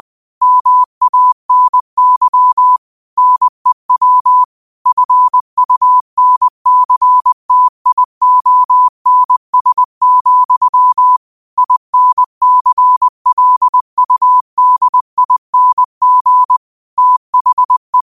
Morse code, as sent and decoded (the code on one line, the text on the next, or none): -- .- -. -.-- / -. .-- / ..-. ..- -. -.-. - .. --- -. ... --..-- / .. -. -.-. .-.. ..- -.. .. -. --. / - .... .
MANY NEW FUNCTIONS, INCLUDING THE